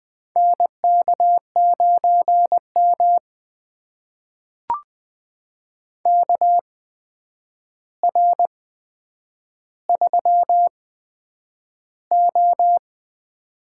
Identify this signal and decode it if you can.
Morse code, heard as a keyed tone, about 20 wpm, spaced as for 9 wpm, 700 Hz. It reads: NK9M KR3O
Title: USA Call Signs - 4 Characters - 8x Character Spacing - 20wpm